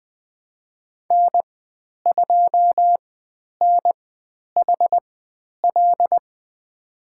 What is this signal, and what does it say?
Morse code, heard as a keyed tone, about 20 wpm, spaced as for 10 wpm, 700 Hz. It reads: N2NHL
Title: 095 Lesson 16g - Letter c - Call Signs 20wpm